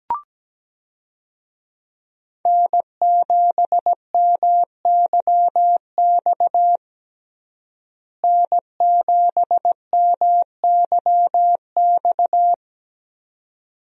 Morse code, heard as a keyed tone, about 17 wpm, 700 Hz. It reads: N7MYX N7MYX
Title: USA Call Signs - 5 Characters - Repeated 2x - 17wpm